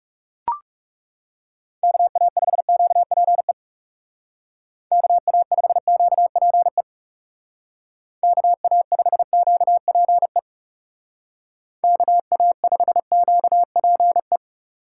Morse code, heard as a keyed tone, about 37 wpm, 700 Hz. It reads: KA5QPE KA5QPE KA5QPE KA5QPE
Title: Speed Racing - USA 2x3 Call Signs 30wpm